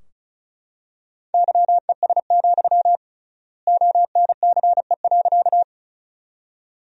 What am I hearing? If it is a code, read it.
Morse code, heard as a keyed tone, about 36 wpm, 700 Hz. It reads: YES, ONCE.
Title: 3x Word-Spacing - Sentences from Top 500 Words - 35wpm